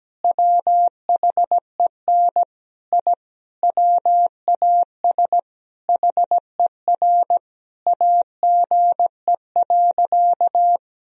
Morse code, heard as a keyed tone, about 17 wpm, 700 Hz. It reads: WHEN I WAS HER AGE.
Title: Sentences from Top 1000 Words - Review Part 3 - 17wpm